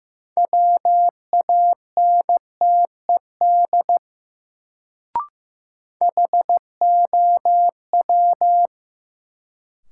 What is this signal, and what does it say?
Morse code, heard as a keyed tone, about 15 wpm, 700 Hz. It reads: WANTED HOW
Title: Sets of 2 Words - New Word Drill - 15wpm